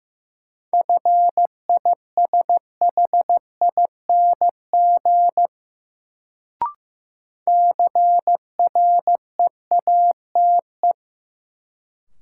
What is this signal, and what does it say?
Morse code, heard as a keyed tone, about 15 wpm, 700 Hz. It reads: FISHING CREATE
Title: Sentences from Top 1600 Words - New Word Drill Alt - 15wpm